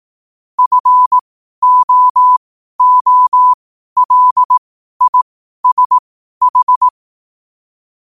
Morse code, heard as a keyed tone, about 18 wpm, spaced as for 12 wpm, 1000 Hz. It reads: FOOLISH
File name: qotdfiles / 12wpm